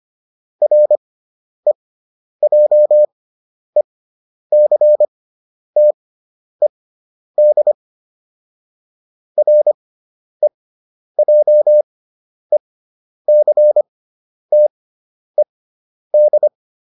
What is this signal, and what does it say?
Morse code, heard as a keyed tone, about 25 wpm, spaced as for 10 wpm, 600 Hz. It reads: REJECTED REJECTED